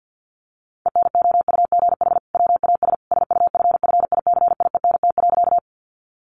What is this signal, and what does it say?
Morse code, heard as a keyed tone, about 50 wpm, 700 Hz. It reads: ENOUGH WAS SUFFICIENT.
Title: Sentences from Top 1400 Words 50wpm